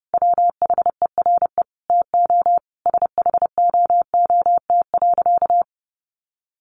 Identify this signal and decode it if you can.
Morse code, heard as a keyed tone, about 30 wpm, 700 Hz. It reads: WHERE TO SHOOT.
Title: A Sound Of Thunder 30wpm